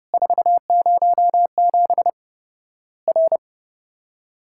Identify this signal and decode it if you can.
Morse code, heard as a keyed tone, about 30 wpm, 700 Hz. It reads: 407 R
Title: Worked All Europe DX Contest - QTC Exchange - 30wpm